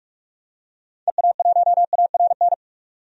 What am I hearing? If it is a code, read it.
Morse code, heard as a keyed tone, about 46 wpm, 700 Hz. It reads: EA1ARN
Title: DXCC-100 Call Signs - 6 Characters - 45wpm